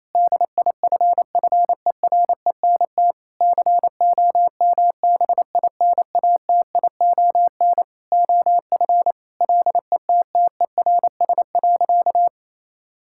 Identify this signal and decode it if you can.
Morse code, heard as a keyed tone, about 28 wpm, 700 Hz. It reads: DIFFERENT COMBINATION OF LETTERS.
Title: Sentences from Top 1000 Words - Review Part 6 - 28wpm